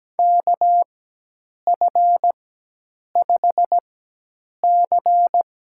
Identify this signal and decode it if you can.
Morse code, heard as a keyed tone, about 17 wpm, spaced as for 8 wpm, 700 Hz. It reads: KF5C
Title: USA Call Signs - 6 Characters - 4x Character Spacing - 17wpm